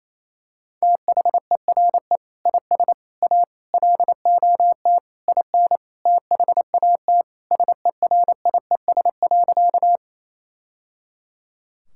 Morse code, heard as a keyed tone, about 28 wpm, 700 Hz. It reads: THERE IS A LOT IN THAT SERIES.
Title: Sentences from Top 1000 Words - Review Part 5 - 28wpm